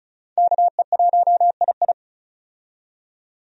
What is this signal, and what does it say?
Morse code, heard as a keyed tone, about 36 wpm, 700 Hz. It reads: KE1II
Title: USA Call Signs - 5 Characters - 35wpm